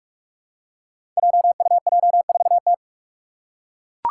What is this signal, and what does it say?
Morse code, heard as a keyed tone, about 46 wpm, 700 Hz. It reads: JUJ4T
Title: Five Challenging Letter-Number Combinations 45wpm